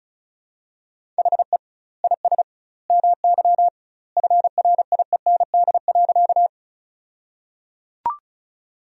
Morse code, heard as a keyed tone, about 34 wpm, 700 Hz. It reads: HE IS MY FRIEND.
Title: 2x Word-Spacing - Sentences from Top 300 Words - 35wpm